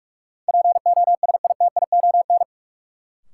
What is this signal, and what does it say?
Morse code, heard as a keyed tone, about 43 wpm, 700 Hz. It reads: POSITION
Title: Sentences from Top 900 Words - New Word Drill - 45wpm